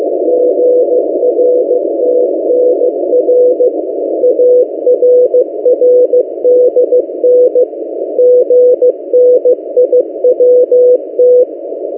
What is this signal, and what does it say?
Morse code, heard as a keyed tone, about 15 wpm, 500 Hz. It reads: GRTNR ARRDN GNIWT
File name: M08a- -1355utc-8096khz